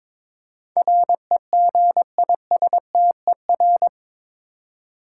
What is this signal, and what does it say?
Morse code, heard as a keyed tone, about 21 wpm, 700 Hz. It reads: REGISTER
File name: practice / 22wpm / Eight Letter Words - 22wpm